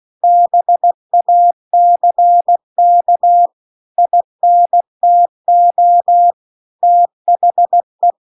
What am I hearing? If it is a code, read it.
Morse code, heard as a keyed tone, about 16 wpm, 700 Hz. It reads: BACK INTO THE